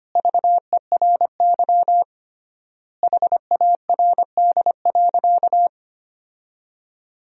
Morse code, heard as a keyed tone, about 25 wpm, 700 Hz. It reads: VERY HARD.
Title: 3x Word-Spacing - Sentences from Top 300 Words - 25wpm